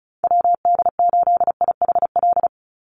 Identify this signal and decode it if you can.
Morse code, heard as a keyed tone, about 34 wpm, 700 Hz. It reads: WD8IHL
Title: USA Call Signs 35wpm